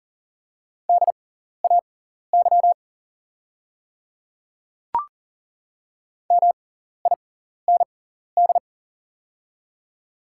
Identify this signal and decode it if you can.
Morse code, heard as a keyed tone, about 40 wpm, spaced as for 14 wpm, 690 Hz. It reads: DAY MIND
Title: 6x Character Spacing - Top 500 Words - 40wpm